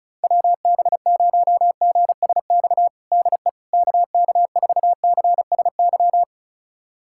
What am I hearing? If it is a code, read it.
Morse code, heard as a keyed tone, about 34 wpm, 700 Hz. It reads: WB0GSX DE KK4CSY
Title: USA Callsign DE Callsign - 6 Characters - 35wpm